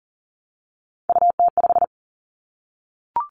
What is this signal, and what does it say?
Morse code, heard as a keyed tone, about 40 wpm, 700 Hz. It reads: UT5
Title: Callsign Prefixes for Top 25 DXCC - 40wpm